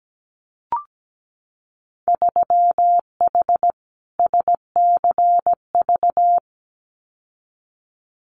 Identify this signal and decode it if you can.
Morse code, heard as a keyed tone, about 17 wpm, 700 Hz. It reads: 3H SCV
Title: Winter Field Day Exchanges Encore 17wpm